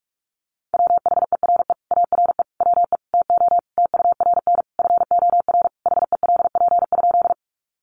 Morse code, heard as a keyed tone, about 44 wpm, 700 Hz. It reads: WHERE ARE WE TO TURN FOR HELP?
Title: Sentences from Top 200 Words 45wpm